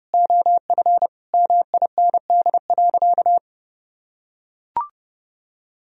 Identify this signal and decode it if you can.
Morse code, heard as a keyed tone, about 30 wpm, 700 Hz. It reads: OF MIND.
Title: Sentences from Top 1000 Words - Review Part 2 - 30wpm